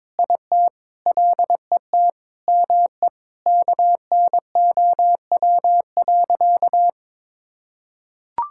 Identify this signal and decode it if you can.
Morse code, heard as a keyed tone, about 22 wpm, 700 Hz. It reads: IT LET ME KNOW.
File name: practice / 22wpm / Sentences with 8-Word Phrases - 22wpm